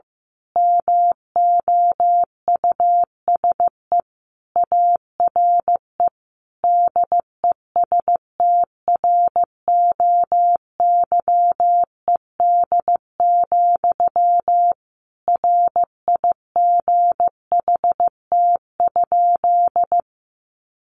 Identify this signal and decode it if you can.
Morse code, heard as a keyed tone, about 15 wpm, 700 Hz. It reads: MOUSE ARE DESTROYED, RIGHT?
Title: A Sound Of Thunder 15wpm